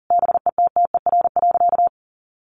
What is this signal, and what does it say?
Morse code, heard as a keyed tone, about 40 wpm, 700 Hz. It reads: BETTER.
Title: Easy Phrases 40wpm